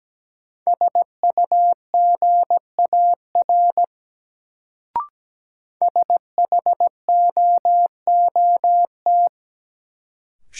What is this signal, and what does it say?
Morse code, heard as a keyed tone, about 17 wpm, 700 Hz. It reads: SUGAR SHOOT
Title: Sentences from Top 1200 Words - New Word Drill - 17wpm